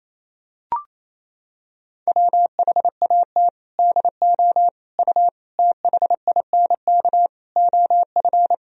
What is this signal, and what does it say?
Morse code, heard as a keyed tone, about 28 wpm, 700 Hz. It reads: WHAT DO U THINK OF
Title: Sentences from Top 200 Words and Abbreviations 28wpm